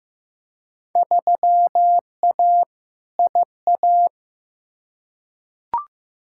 Morse code, heard as a keyed tone, about 15 wpm, 700 Hz. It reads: 3A IA
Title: Field Day Exchanges Encore - 15wpm